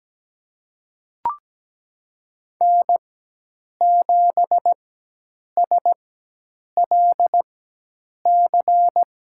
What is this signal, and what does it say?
Morse code, heard as a keyed tone, about 17 wpm, spaced as for 8 wpm, 700 Hz. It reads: N7SLC
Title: USA Call Signs - 5 Characters - 4x Character Spacing - 17wpm